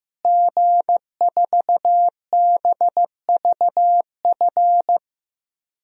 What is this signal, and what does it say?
Morse code, heard as a keyed tone, about 16 wpm, 700 Hz. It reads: G4BVF
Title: DXCC-100 Call Signs - 5 Characters - Repeated 3x - 15wpm